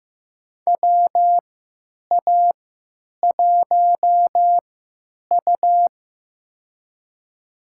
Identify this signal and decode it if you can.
Morse code, heard as a keyed tone, about 15 wpm, spaced as for 9 wpm, 700 Hz. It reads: WA1U
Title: USA Call Signs - 4 Characters - 3x Character Spacing - 15wpm